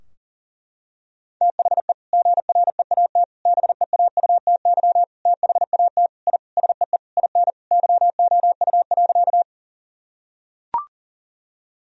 Morse code, heard as a keyed tone, about 37 wpm, 700 Hz. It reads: THE GREAT BEAUTY THAT I SEE IN YOU.
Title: Sentences from Top 1000 Words - Review Part 2 - 40wpm